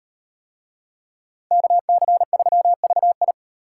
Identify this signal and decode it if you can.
Morse code, heard as a keyed tone, about 38 wpm, 700 Hz. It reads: KC3VI